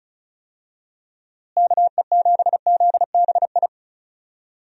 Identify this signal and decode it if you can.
Morse code, heard as a keyed tone, about 36 wpm, 700 Hz. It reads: KE7ZBI
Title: USA Call Signs - 6 Characters - Repeated 2x - 35wpm